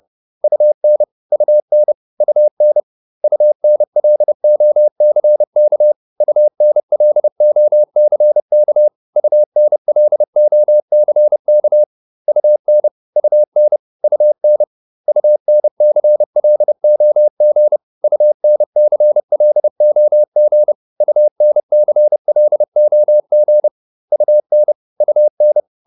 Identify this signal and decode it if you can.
Morse code, heard as a keyed tone, about 30 wpm, 600 Hz. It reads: UN UN UN UNLOCK UNLOCK UNLOCK UN UN UN UNCLOG UNCLOG UNCLOG UN UN
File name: UN6 30